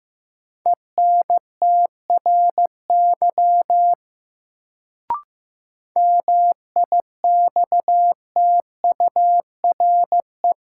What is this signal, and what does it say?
Morse code, heard as a keyed tone, about 15 wpm, 700 Hz. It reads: ENTRY MIXTURE
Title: Sentences from Top 1500 Words - New Word Drill Alt - 15wpm